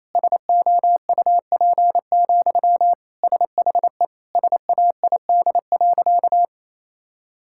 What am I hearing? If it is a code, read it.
Morse code, heard as a keyed tone, about 28 wpm, 700 Hz. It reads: SOUP, SHE SAID.
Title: Sentences from Top 1000 Words - Review Part 6 - 28wpm